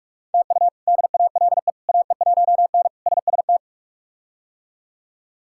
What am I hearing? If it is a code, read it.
Morse code, heard as a keyed tone, about 46 wpm, 700 Hz. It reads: TU DALE AE1N SST
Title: K1USN SST QSO Practice - 45wpm